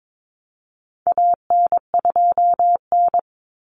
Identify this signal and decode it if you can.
Morse code, heard as a keyed tone, about 22 wpm, 700 Hz. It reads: AN2N
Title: Common Cut Numbers - 4-Digit 22wpm